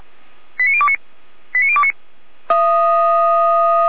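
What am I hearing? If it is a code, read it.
Five-tone selective call (ZVEI), heard as quick runs of five tones, tones of about 70 ms. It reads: 89029 89029
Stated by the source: ZVEI five-tone selective call received by SDR, from the feuerwehr-orlamuende.org fire brigade audio server.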